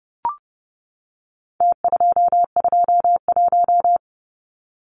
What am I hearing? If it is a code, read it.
Morse code, heard as a keyed tone, about 30 wpm, 700 Hz. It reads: T221
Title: Numbers - Contest Style - 4-Digit 30wpm